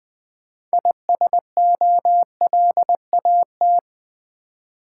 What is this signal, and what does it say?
Morse code, heard as a keyed tone, about 20 wpm, 700 Hz. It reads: ISOLAT
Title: Build A Word - 3 Letters At A Time - 9-Letter Words - 20wpm